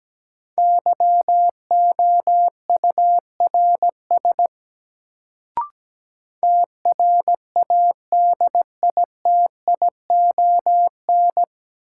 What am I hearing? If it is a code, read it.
Morse code, heard as a keyed tone, about 17 wpm, 700 Hz. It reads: YOURS TRADITION
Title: Sentences from Top 1500 Words - New Word Drill - 17wpm